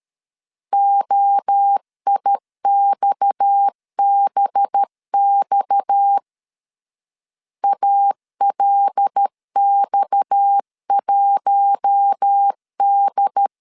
Morse code, heard as a keyed tone, about 13 wpm, 800 Hz. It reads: OIXBX ALX1D
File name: LEZIONI / 6 PKX0 / 6 PKX 12wpm 2024